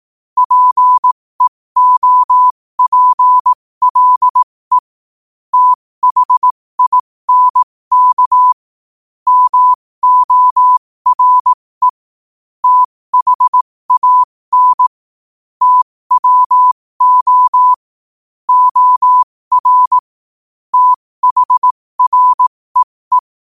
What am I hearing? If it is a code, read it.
Morse code, heard as a keyed tone, about 18 wpm, spaced as for 15 wpm, 1000 Hz. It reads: PEOPLE THINK MORE THAN TWO OR THREE
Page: New quotes every day in morse code at 15 Words per minute.